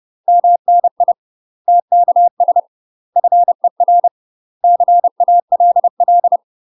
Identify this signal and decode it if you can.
Morse code, heard as a keyed tone, about 30 wpm, 700 Hz. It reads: MNI TKS FER CALL
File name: UK MASTER QSOs TRAINING / 30WPM / UK MASTER QSOs 0031